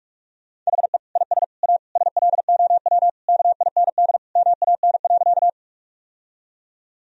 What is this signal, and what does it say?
Morse code, heard as a keyed tone, about 46 wpm, 700 Hz. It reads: HE IS A SLOW KIND MAN.
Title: Sentences from Top 1000 Words - Review Part 2 - 45wpm